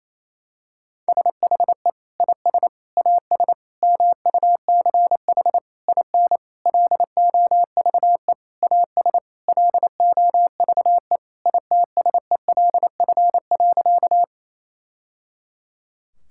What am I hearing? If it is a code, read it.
Morse code, heard as a keyed tone, about 28 wpm, 700 Hz. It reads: SHE IS AS MUCH IN LOVE AS LOVE ITSELF.
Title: Sentences from Top 1000 Words - Review Part 5 - 28wpm